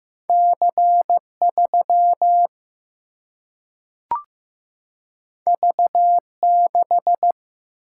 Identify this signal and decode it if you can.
Morse code, heard as a keyed tone, about 15 wpm, 700 Hz. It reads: C3 V6
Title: Single-Letter Single-Number 15wpm